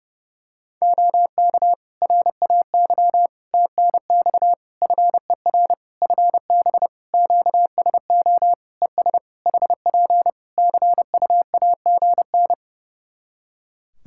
Morse code, heard as a keyed tone, about 30 wpm, 700 Hz. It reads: OK RAY TNX FER FB QSO ES HP CUAGN
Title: Practice QSO - Exchange 4 - Ending - 30wpm